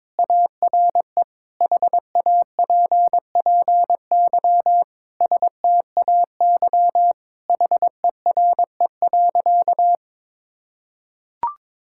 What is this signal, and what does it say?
Morse code, heard as a keyed tone, about 22 wpm, 700 Hz. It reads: ARE HAPPY STAY HERE.
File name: practice / 22wpm / Sentences with 4-Word Phrases - 22wpm